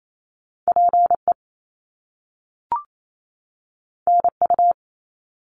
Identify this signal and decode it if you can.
Morse code, heard as a keyed tone, about 28 wpm, 700 Hz. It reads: PE NU
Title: Canadian Provinces and Territories 28wpm